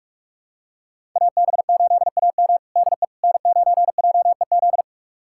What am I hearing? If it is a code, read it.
Morse code, heard as a keyed tone, about 47 wpm, 700 Hz. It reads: AB8AM DE N9JEZ
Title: USA Callsign DE Callsign - 5 Characters - 45wpm